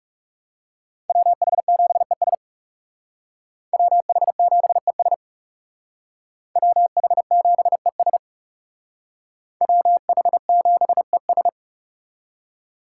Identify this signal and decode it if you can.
Morse code, heard as a keyed tone, about 36 wpm, 700 Hz. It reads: WH7ES WH7ES WH7ES WH7ES
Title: Speed Racing - USA 2x2 Call Signs 30wpm